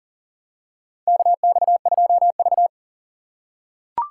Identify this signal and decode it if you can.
Morse code, heard as a keyed tone, about 40 wpm, 700 Hz. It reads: KX2V
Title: USA Call Signs - 4 Characters - 6x Character Spacing - 40wpm